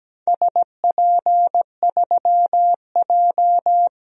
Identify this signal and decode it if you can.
Morse code, heard as a keyed tone, about 17 wpm, 700 Hz. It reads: SP3J
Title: DXCC-100 Call Signs - 4 Characters - Repeated 3x - 17wpm